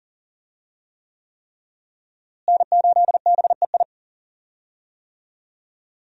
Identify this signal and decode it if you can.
Morse code, heard as a keyed tone, about 40 wpm, 700 Hz. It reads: N8BEI
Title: USA Call Signs - 5 Characters - Repeated 3x - 40wpm